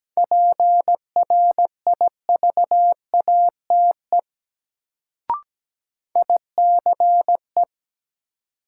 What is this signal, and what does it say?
Morse code, heard as a keyed tone, about 17 wpm, 700 Hz. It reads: PRIVATE ICE
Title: Sentences from Top 1000 Words - New Word Drill - 17wpm